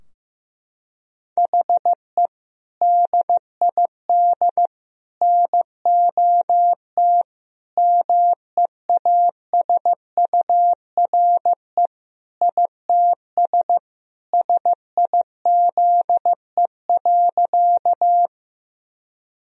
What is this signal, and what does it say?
Morse code, heard as a keyed tone, about 15 wpm, 700 Hz. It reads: HE DID NOT MEASURE ITS SIZE.
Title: Sentences from Top 1000 Words - Review Part 2 - 15wpm